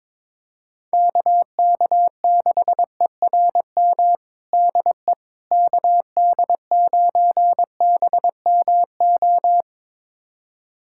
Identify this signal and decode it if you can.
Morse code, heard as a keyed tone, about 22 wpm, 700 Hz. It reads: KK6ERM DE KD9BMO
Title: USA Callsign DE Callsign - 6 Characters - 22wpm